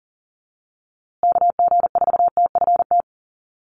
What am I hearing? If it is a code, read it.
Morse code, heard as a keyed tone, about 40 wpm, 700 Hz. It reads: KG4TFT
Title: USA Call Signs Repeated 2x 40wpm